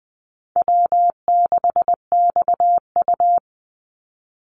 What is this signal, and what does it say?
Morse code, heard as a keyed tone, about 20 wpm, 700 Hz. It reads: W6XU
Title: Easy Call Signs - 20wpm